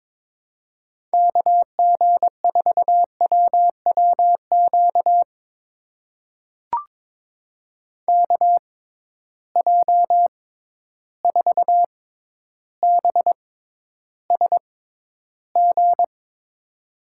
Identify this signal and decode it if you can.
Morse code, heard as a keyed tone, about 21 wpm, spaced as for 11 wpm, 700 Hz. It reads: KG4WWQ KJ4BSG
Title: USA Call Signs - 6 Characters - 6x Character Spacing - 22wpm